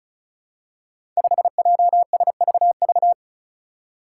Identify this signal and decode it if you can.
Morse code, heard as a keyed tone, about 36 wpm, 700 Hz. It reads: 5JSVV
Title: Five Challenging Letter-Number Combinations 35wpm